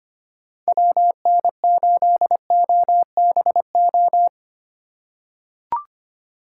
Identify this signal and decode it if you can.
Morse code, heard as a keyed tone, about 25 wpm, 700 Hz. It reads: WN8OBO
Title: USA Call Signs - 6 Characters - 4x Character Spacing - 25wpm